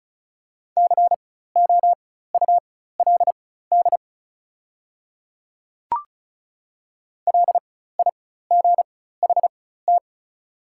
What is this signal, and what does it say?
Morse code, heard as a keyed tone, about 34 wpm, spaced as for 16 wpm, 700 Hz. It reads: COULD LIGHT
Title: 4x Character Spacing - Top 200 Words - 35wpm